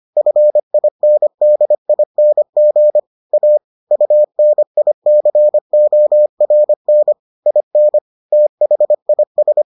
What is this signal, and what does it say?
Morse code, heard as a keyed tone, about 25 wpm, 600 Hz. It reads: FINDING A UNICORN IN THIS